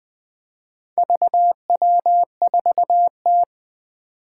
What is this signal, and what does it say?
Morse code, heard as a keyed tone, about 20 wpm, 700 Hz. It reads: VW4T
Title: Four Challenging Letter-Number Combinations 20wpm